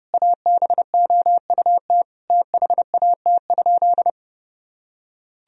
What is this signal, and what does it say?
Morse code, heard as a keyed tone, about 32 wpm, 700 Hz. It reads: ABOUT THAT?
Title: Sentences from Top 1000 Words - Review Part 4 - 30wpm